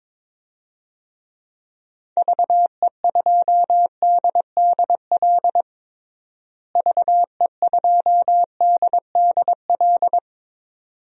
Morse code, heard as a keyed tone, about 21 wpm, 700 Hz. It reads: VE2DDL VE2DDL
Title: DXCC-100 Call Signs - 6 Characters - Repeated 2x - 22wpm